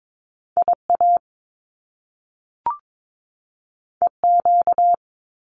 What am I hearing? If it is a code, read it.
Morse code, heard as a keyed tone, about 22 wpm, 700 Hz. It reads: IA EQ
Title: Top Two Letter Combinations in Words 22wpm